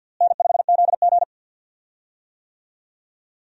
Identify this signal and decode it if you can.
Morse code, heard as a keyed tone, about 53 wpm, 700 Hz. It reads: N5ZG
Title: USA Call Signs - 4 Characters - Repeated 2x - 50wpm